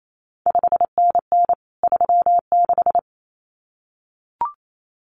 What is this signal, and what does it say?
Morse code, heard as a keyed tone, about 28 wpm, 700 Hz. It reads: 5NN 36
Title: CQ WW DX Contest Exchange Encore - 28wpm